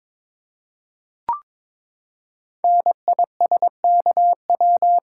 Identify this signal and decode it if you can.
Morse code, heard as a keyed tone, about 22 wpm, 700 Hz. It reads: NISKW
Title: Code Groups - 5 Letters - 22wpm